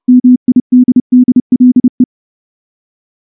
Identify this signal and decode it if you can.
Morse code, heard as a keyed tone, about 28 wpm, 250 Hz. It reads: MIDDLE
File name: morse-training-30WPM